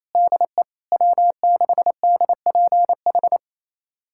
Morse code, heard as a keyed tone, about 28 wpm, 700 Hz. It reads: DE W6DPH
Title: USA Callsign DE Callsign - 5 Characters - 28wpm